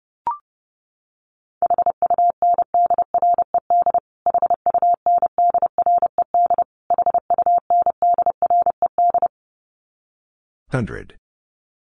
Morse code, heard as a keyed tone, about 30 wpm, 700 Hz. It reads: HUNDRED HUNDRED HUNDRED
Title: Top 500 Words Repeated 30wpm